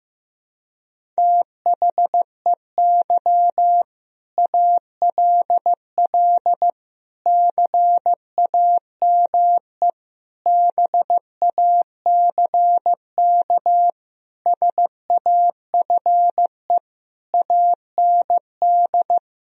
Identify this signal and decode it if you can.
Morse code, heard as a keyed tone, about 15 wpm, 700 Hz. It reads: THEY ALL CAME BACK SAFE AND
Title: Sentences from Top 1000 Words - Review Part 1 - 15wpm